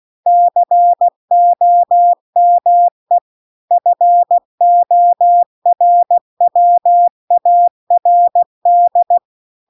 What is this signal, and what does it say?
Morse code, heard as a keyed tone, about 16 wpm, 700 Hz. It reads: COME FORWARD